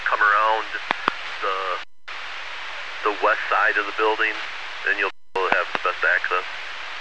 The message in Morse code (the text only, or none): none